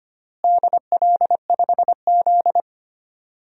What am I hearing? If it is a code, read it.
Morse code, heard as a keyed tone, about 25 wpm, 700 Hz. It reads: DL5Z
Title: DXCC-100 Call Signs - 4 Characters - Repeated 2x - 25wpm